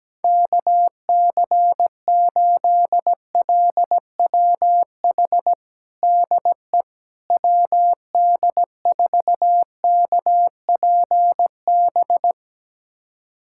Morse code, heard as a keyed tone, about 17 wpm, 700 Hz. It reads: KC8LWH DE WD4KPB
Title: USA Callsign DE Callsign 17wpm